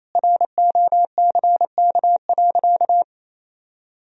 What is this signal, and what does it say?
Morse code, heard as a keyed tone, about 28 wpm, 700 Hz. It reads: ROCK.
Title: Sentences from Top 1000 Words - Review Part 2 - 28wpm